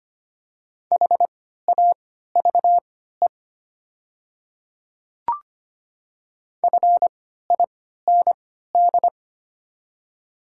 Morse code, heard as a keyed tone, about 25 wpm, spaced as for 14 wpm, 700 Hz. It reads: HAVE FIND
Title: 3x Character Spacing - Top 100 Words - 25wpm